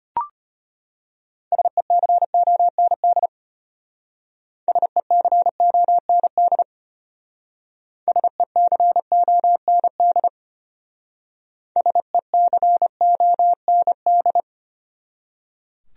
Morse code, heard as a keyed tone, about 32 wpm, 700 Hz. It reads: SECOND SECOND SECOND SECOND